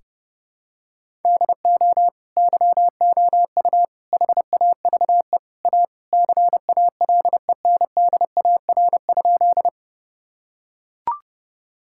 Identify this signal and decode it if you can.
Morse code, heard as a keyed tone, about 28 wpm, 700 Hz. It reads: DO YOU HAVE A CALENDAR?
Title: Sentences from Top 1600 Words 30wpm